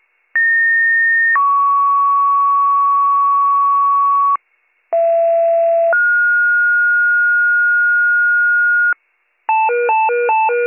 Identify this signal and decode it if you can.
Two-tone page, heard as a one-second tone, then a three-second tone, about 1850 then 1120 Hz, about 670 then 1430 Hz, with noise. It is heard from 0.3 to 8.9 s.